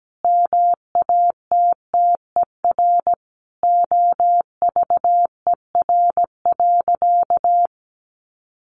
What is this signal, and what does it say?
Morse code, heard as a keyed tone, about 17 wpm, 700 Hz. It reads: MATTER OVER.
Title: Sentences from Top 700 Words 17wpm